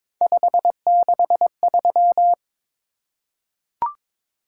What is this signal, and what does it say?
Morse code, heard as a keyed tone, about 22 wpm, 700 Hz. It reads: 563
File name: practice / 22wpm / Three Letter-Number Combinations - 22wpm